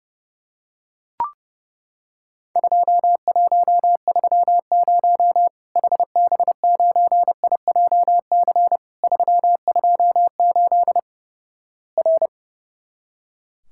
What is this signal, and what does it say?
Morse code, heard as a keyed tone, about 31 wpm, 700 Hz. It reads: 2130 HB9IJC 328 R
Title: Worked All Europe DX Contest - QTC Exchange - 30wpm